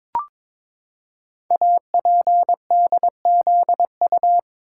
Morse code, heard as a keyed tone, about 23 wpm, 700 Hz. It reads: APDZU
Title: Code Groups - 5 Letters - 22wpm